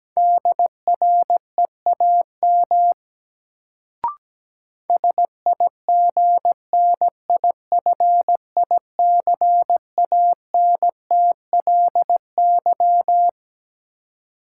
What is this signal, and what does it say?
Morse code, heard as a keyed tone, about 17 wpm, 700 Hz. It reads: DREAM SIGNIFICANTLY
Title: Sentences from Top 1500 Words - New Word Drill - 17wpm